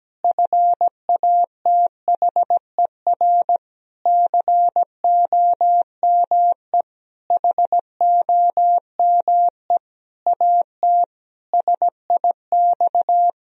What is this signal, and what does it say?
Morse code, heard as a keyed tone, about 17 wpm, 700 Hz. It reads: FATHER COME HOME AT SIX
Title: Sentences from Top 1000 Words - Review Part 2 - 17wpm